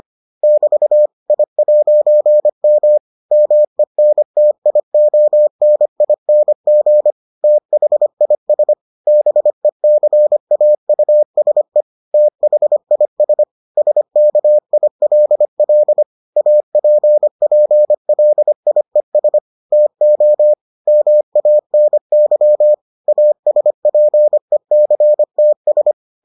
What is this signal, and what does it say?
Morse code, heard as a keyed tone, about 25 wpm, 600 Hz. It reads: <BT> I'M MENTIONING THIS BECAUSE THIS SKILL APPLIES TO MANY ASPECTS